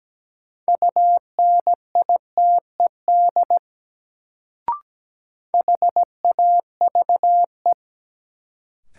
Morse code, heard as a keyed tone, about 17 wpm, 700 Hz. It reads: UNITED HAVE
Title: Sets of 2 Words - New Word Drill - 17wpm